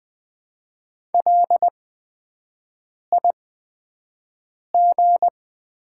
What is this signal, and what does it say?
Morse code, heard as a keyed tone, about 20 wpm, spaced as for 5 wpm, 700 Hz. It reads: LIG
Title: 8x Character Spacing - Top 200 Words - 20wpm